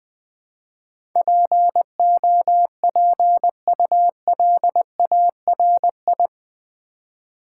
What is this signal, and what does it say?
Morse code, heard as a keyed tone, about 20 wpm, 700 Hz. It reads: POPULARI
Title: Build A Word - 4 Letters At A Time - 10-Letter Words - 20wpm